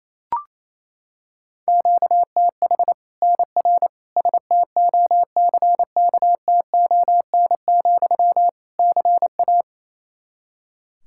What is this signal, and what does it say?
Morse code, heard as a keyed tone, about 28 wpm, 700 Hz. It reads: QTH NR STOCKTON, CA
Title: Practice QSO - Exchange 1 - Beginning - 28wpm